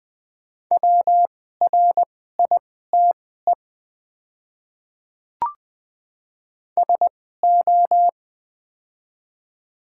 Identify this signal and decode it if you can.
Morse code, heard as a keyed tone, about 20 wpm, spaced as for 15 wpm, 700 Hz. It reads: WRITE SO
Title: 2x Character Spacing - Top 100 Words - 20wpm